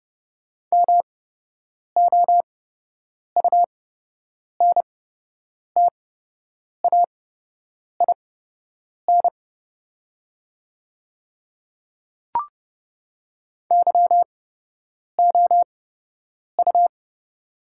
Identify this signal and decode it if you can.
Morse code, heard as a keyed tone, about 30 wpm, spaced as for 8 wpm, 700 Hz. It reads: MOUNTAIN YOU
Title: 8x Character Spacing - Top 300 Words - 30wpm